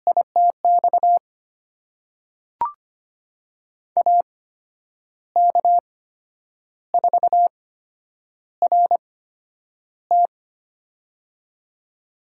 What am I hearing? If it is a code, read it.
Morse code, heard as a keyed tone, about 25 wpm, spaced as for 9 wpm, 700 Hz. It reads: ITX AK4RT
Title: USA Call Signs - 8x Character Spacing - 25wpm